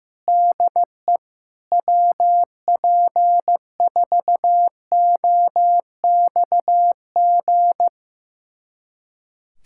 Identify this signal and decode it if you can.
Morse code, heard as a keyed tone, about 16 wpm, 700 Hz. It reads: DE WP4OXG
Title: USA Callsign DE Callsign - 6 Characters - 15wpm